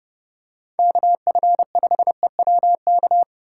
Morse code, heard as a keyed tone, about 30 wpm, 700 Hz. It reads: KF5EWK